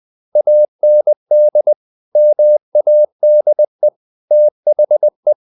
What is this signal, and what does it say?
Morse code, heard as a keyed tone, about 20 wpm, 600 Hz. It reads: AND MADE THE
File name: the old man and the sea part 2 20wpm